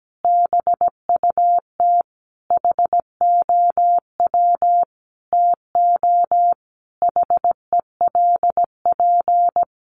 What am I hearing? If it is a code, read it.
Morse code, heard as a keyed tone, about 17 wpm, 700 Hz. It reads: BUT HOW TO HELP
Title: Sentences from Top 1000 Words 17wpm